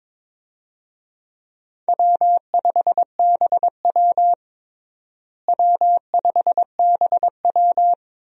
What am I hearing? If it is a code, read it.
Morse code, heard as a keyed tone, about 21 wpm, 700 Hz. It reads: W5BW W5BW
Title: USA Call Signs - 4 Characters - Repeated 2x - 22wpm